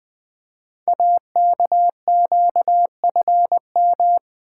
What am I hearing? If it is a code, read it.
Morse code, heard as a keyed tone, about 21 wpm, 700 Hz. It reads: AKQFM
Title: Code Groups - 5 Letter:Numbers - 20wpm